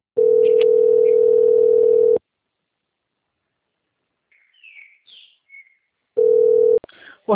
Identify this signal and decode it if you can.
Ringback tone (North America), 440 + 480 Hz.